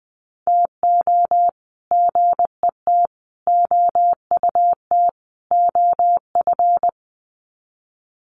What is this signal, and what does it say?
Morse code, heard as a keyed tone, about 20 wpm, 700 Hz. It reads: TO GET OUT OF
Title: Sets of 4 Words 20wpm